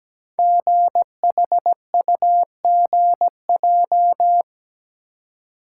Morse code, heard as a keyed tone, about 17 wpm, 700 Hz. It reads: GHUGJ
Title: Code Groups - 5 Letters - 17wpm